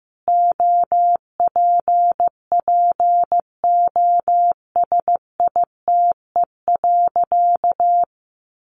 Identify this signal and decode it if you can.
Morse code, heard as a keyed tone, about 15 wpm, 700 Hz. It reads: OPPOSITE.
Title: Sentences from Top 1300 Words 15wpm